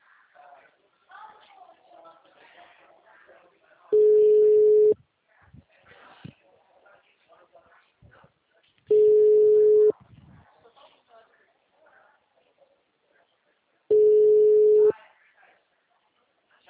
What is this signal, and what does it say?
Ringback tone (Europe, 425 Hz), with speech.
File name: out- -2013- - - .14